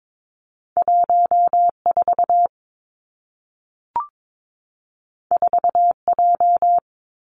Morse code, heard as a keyed tone, about 22 wpm, 700 Hz. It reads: 1 4 4 J
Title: Two Challenging Letter-Number Combinations 22wpm